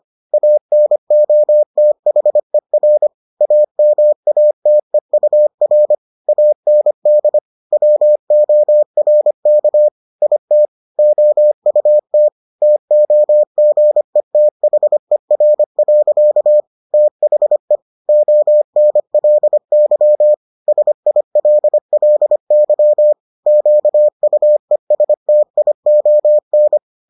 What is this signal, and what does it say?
Morse code, heard as a keyed tone, about 25 wpm, 600 Hz. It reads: ANOTHER AMATEUR AND WORK IT OUT TOGETHER. THE ONLY SILLY QUESTION